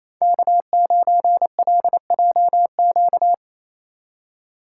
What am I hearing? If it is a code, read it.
Morse code, heard as a keyed tone, about 28 wpm, 700 Hz. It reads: K9LJQ
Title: USA Call Signs - 6x Character Spacing - 28wpm